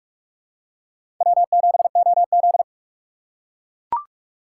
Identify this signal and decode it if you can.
Morse code, heard as a keyed tone, about 44 wpm, 690 Hz. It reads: W7OZ
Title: USA Call Signs - 4 Characters - 3x Character Spacing - 45wpm